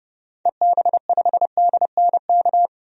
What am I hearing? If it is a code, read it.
Morse code, heard as a keyed tone, about 32 wpm, 700 Hz. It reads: EB5DNK
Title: DXCC-100 Call Signs - 6 Characters - 30wpm